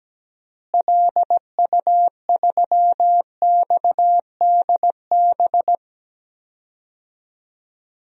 Morse code, heard as a keyed tone, about 17 wpm, 700 Hz. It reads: LU3XDB
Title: DXCC-100 Call Signs - 6 Characters - Repeated 3x - 17wpm